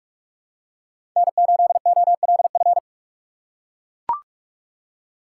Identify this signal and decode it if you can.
Morse code, heard as a keyed tone, about 42 wpm, 700 Hz. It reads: N8OLF
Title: USA Call Signs - 5 Characters - 4x Character Spacing - 45wpm